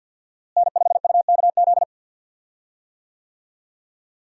Morse code, heard as a keyed tone, about 49 wpm, 700 Hz. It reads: N5UKZ
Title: USA Call Signs - 5 Characters - Repeated 2x - 50wpm